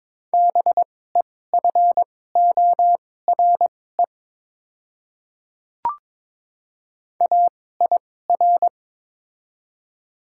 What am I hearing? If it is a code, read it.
Morse code, heard as a keyed tone, about 22 wpm, spaced as for 16 wpm, 700 Hz. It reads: BEFORE AIR
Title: 2x Character Spacing - Top 200 Words - 22wpm